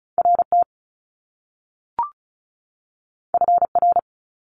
Morse code, heard as a keyed tone, about 36 wpm, 700 Hz. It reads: RT FR
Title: Top Two Letter Combinations in Words 35wpm